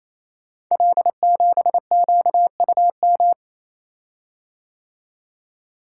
Morse code, heard as a keyed tone, about 27 wpm, 700 Hz. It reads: L7QUM
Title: Code Groups - 5 Letter:Numbers - 28wpm